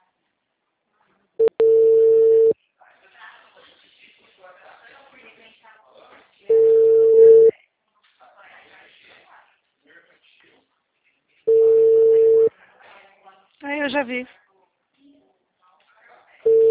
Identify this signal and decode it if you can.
Ringback tone (China, 450 Hz), with speech.